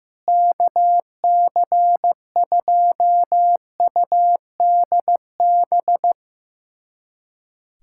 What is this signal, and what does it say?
Morse code, heard as a keyed tone, about 16 wpm, 700 Hz. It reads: KC2UDB
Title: USA Callsign DE Callsign - 6 Characters - 15wpm